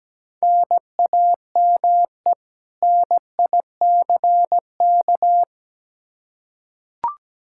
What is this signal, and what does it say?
Morse code, heard as a keyed tone, about 17 wpm, 700 Hz. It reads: NAME NICK
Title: Practice QSO - Exchange 1 - Beginning - 17wpm